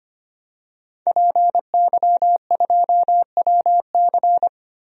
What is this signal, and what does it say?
Morse code, heard as a keyed tone, about 25 wpm, 700 Hz. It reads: PY2WC
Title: DXCC-100 Call Signs - 5 Characters - Repeated 2x - 25wpm